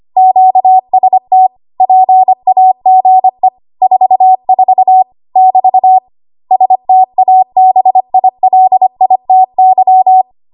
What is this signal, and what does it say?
Morse code, heard as a keyed tone, about 25 wpm, 750 Hz. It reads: QST PAGE 44 <BT> STABILITY
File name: Archive / 25 WPM / 25WPM